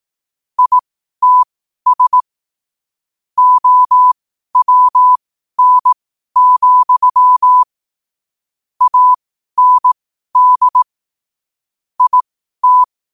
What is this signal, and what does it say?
Morse code, heard as a keyed tone, about 18 wpm, spaced as for 12 wpm, 1000 Hz. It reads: ITS OWN, AND IT
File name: qotdfiles / 12wpm